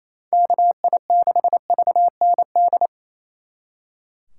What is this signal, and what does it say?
Morse code, heard as a keyed tone, about 34 wpm, 700 Hz. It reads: KI6VND
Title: Speed Racing - USA 2x3 Call Signs 28wpm